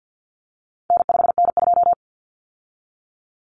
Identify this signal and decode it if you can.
Morse code, heard as a keyed tone, about 50 wpm, 700 Hz. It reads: N5N2
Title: Common Cut Numbers - 4-Digit 50wpm